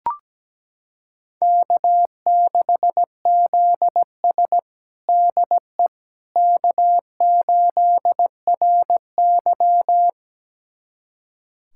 Morse code, heard as a keyed tone, about 17 wpm, 700 Hz. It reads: K6ZS DE K8RY
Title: USA Callsign DE Callsign - 4 Characters - 17wpm